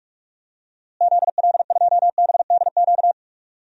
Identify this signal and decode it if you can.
Morse code, heard as a keyed tone, about 46 wpm, 700 Hz. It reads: ZL2BDQ
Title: DXCC-100 Call Signs - 6 Characters - Repeated 2x - 45wpm